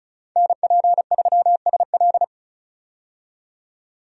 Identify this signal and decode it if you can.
Morse code, heard as a keyed tone, about 36 wpm, 700 Hz. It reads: NP3SL
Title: USA Call Signs - 5 Characters - 35wpm